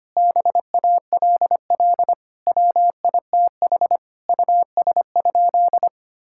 Morse code, heard as a keyed tone, about 25 wpm, 700 Hz. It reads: BALL WITH US?
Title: Sentences from Top 1000 Words - Review Part 6 - 25wpm